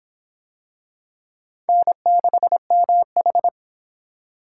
Morse code, heard as a keyed tone, about 21 wpm, 700 Hz. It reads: N6MH
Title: Speed Racing - USA 1x2 Call Signs 17wpm